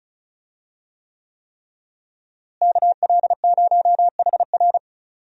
Morse code, heard as a keyed tone, about 36 wpm, 700 Hz. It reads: KL0HR
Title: USA Call Signs - 5 Characters - Repeated 3x - 35wpm